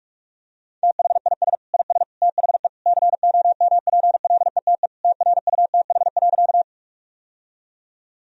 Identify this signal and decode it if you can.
Morse code, heard as a keyed tone, about 44 wpm, 700 Hz. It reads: THIS IS THE COMPLETE TRUTH.
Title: Sentences from Top 1000 Words - Review Part 2 - 45wpm